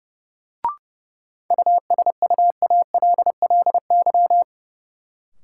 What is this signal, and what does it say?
Morse code, heard as a keyed tone, about 31 wpm, 700 Hz. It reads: USUALLY
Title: Sentences from Top 1000 Words - Review Part 3 - New Word Drill - 30wpm